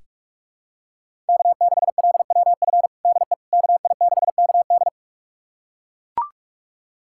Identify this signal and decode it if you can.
Morse code, heard as a keyed tone, about 41 wpm, 700 Hz. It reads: K6LWF DE KI6KD
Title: USA Callsign DE Callsign - 5 Characters - 45wpm